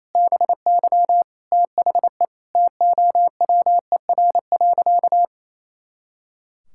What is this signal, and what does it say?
Morse code, heard as a keyed tone, about 28 wpm, 700 Hz. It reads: BY THE TOWER.
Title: Pangram Challenge v2 - 28wpm